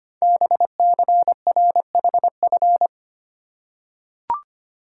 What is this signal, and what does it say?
Morse code, heard as a keyed tone, about 25 wpm, 700 Hz. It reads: BCRHF
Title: Code Groups - 5 Letters - 25wpm